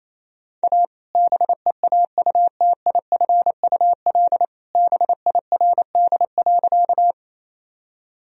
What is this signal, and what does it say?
Morse code, heard as a keyed tone, about 27 wpm, 700 Hz. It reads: A BEAUTIFUL BIRD.
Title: Sentences from Top 1000 Words - Review Part 2 - 28wpm